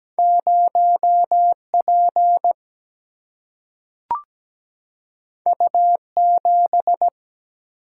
Morse code, heard as a keyed tone, about 17 wpm, 700 Hz. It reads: 0P U7
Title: Single-Letter Single-Number 17wpm